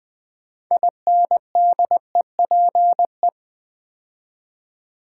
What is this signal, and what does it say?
Morse code, heard as a keyed tone, about 20 wpm, 700 Hz. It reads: INDEPE
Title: Build A Word - 6 Letters At A Time - 12-Letter Words - 20wpm